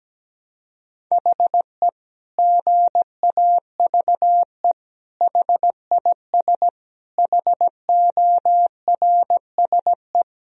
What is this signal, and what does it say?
Morse code, heard as a keyed tone, about 17 wpm, 700 Hz. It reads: HE GAVE HIS HORSE
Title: Sentences from Top 1000 Words - Review Part 1 - 17wpm